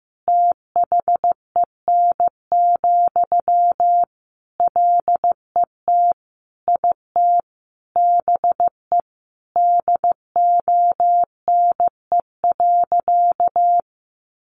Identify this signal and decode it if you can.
Morse code, heard as a keyed tone, about 15 wpm, 700 Hz. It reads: THEN, LET IT BE DONE.
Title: Sentences from Top 500 Words 15wpm